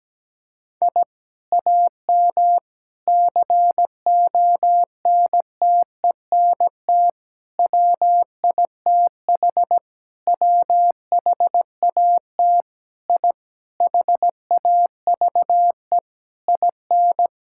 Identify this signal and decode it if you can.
Morse code, heard as a keyed tone, about 17 wpm, 700 Hz. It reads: I AM CONTENT WITH WHAT I HAVE IN
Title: Sentences from Top 1000 Words - Review Part 6 - 17wpm